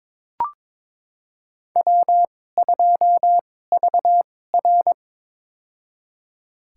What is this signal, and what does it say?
Morse code, heard as a keyed tone, about 22 wpm, spaced as for 16 wpm, 700 Hz. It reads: W2VR
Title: USA Call Signs - 4 Characters - 2x Character Spacing - 22wpm